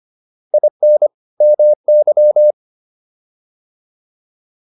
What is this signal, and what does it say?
Morse code, heard as a keyed tone, about 25 wpm, 600 Hz. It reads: IN MY